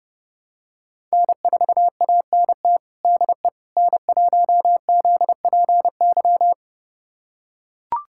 Morse code, heard as a keyed tone, about 32 wpm, 700 Hz. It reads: N4ANT DE N1ZPY
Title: USA Callsign DE Callsign - 5 Characters - 30wpm